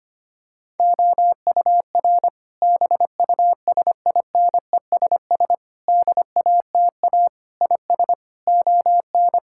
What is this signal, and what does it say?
Morse code, heard as a keyed tone, about 25 wpm, 700 Hz. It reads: OUR BUSINESS DATA IS ON
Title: Sentences from Top 1000 Words - Review Part 6 - 25wpm